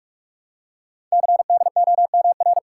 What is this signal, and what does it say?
Morse code, heard as a keyed tone, about 44 wpm, 700 Hz. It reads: CDOMR
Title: Code Groups - 5 Letters - 45wpm